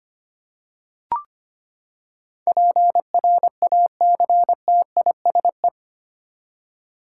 Morse code, heard as a keyed tone, about 25 wpm, 700 Hz. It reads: PRACTISE
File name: Eight Letter Words - 25wpm